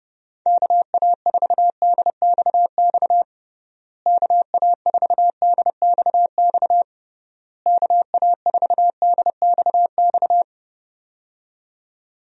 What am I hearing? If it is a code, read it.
Morse code, heard as a keyed tone, about 30 wpm, 700 Hz. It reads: KA4DXX KA4DXX KA4DXX
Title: USA Call Signs - 6 Characters - Repeated 3x - 30wpm